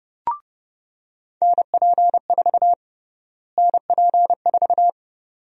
Morse code, heard as a keyed tone, about 32 wpm, 700 Hz. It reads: NP4 NP4
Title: USA Call Sign Prefixes - Repeated 2x - 30wpm